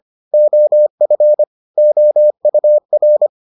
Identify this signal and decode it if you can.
Morse code, heard as a keyed tone, about 25 wpm, 600 Hz. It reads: OF OUR